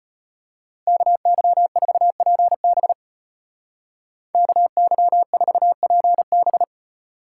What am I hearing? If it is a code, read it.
Morse code, heard as a keyed tone, about 33 wpm, 700 Hz. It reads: KY4PB KY4PB